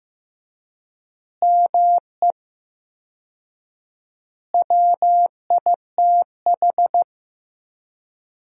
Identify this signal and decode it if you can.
Morse code, heard as a keyed tone, about 15 wpm, 700 Hz. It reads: ME WITH
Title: 4x Word-Spacing - Sentences from Top 500 Words - 15wpm